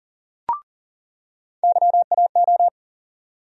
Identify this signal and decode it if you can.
Morse code, heard as a keyed tone, about 40 wpm, 700 Hz. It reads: YAO
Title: USA Call Sign Postfixes - 3 Letters 40wpm